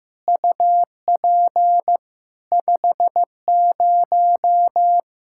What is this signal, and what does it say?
Morse code, heard as a keyed tone, about 16 wpm, 700 Hz. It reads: UP 50
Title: Practice QSO - Exchange 2 - Info - 15wpm